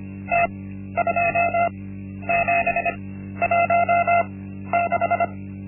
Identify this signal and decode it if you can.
Morse code, heard as a keyed tone, about 25 wpm, spaced as for 12 wpm, 690 Hz. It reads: T2716
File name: Linksmakalnis-1